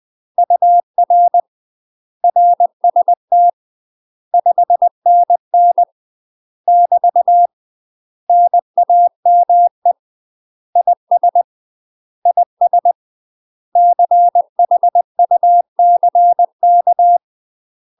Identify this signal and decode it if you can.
Morse code, heard as a keyed tone, about 20 wpm, 700 Hz. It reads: UR RST 5NN <BT> NAME IS IS CHUCK
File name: UK MASTER QSOs TRAINING / 20WPM / UK MASTER QSOs 0020